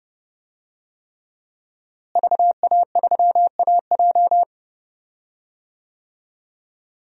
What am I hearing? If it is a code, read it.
Morse code, heard as a keyed tone, about 30 wpm, 700 Hz. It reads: VA3AJ
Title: DXCC-100 Call Signs - 5 Characters - Repeated 3x - 30wpm